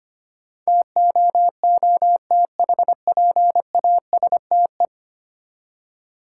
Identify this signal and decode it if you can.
Morse code, heard as a keyed tone, about 25 wpm, 690 Hz. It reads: TOOTHPASTE
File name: practice / 25wpm / Easy Words with Hints - 25wpm